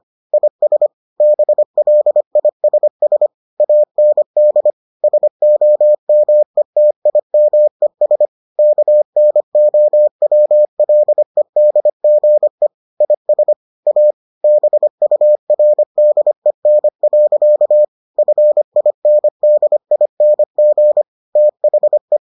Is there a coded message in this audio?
Yes: Morse code from 0.3 to 22.2 s, heard as a keyed tone, about 25 wpm, 600 Hz. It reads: IS BLISS AND SOMETIMES KNOWLEDGE IS A BURDEN. FINDING THE